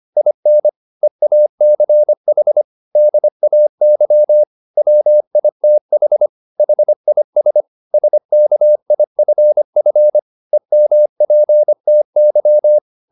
Morse code, heard as a keyed tone, about 25 wpm, 600 Hz. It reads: IN EACH DAY WITH HIS SKIFF EMPTY
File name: the old man and the sea part 1 25wpm